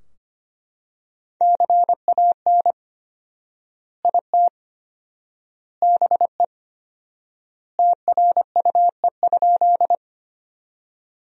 Morse code, heard as a keyed tone, about 25 wpm, 700 Hz. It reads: CAN IT BE TRUE?
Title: 4x Word-Spacing - Sentences from Top 400 Words - 25wpm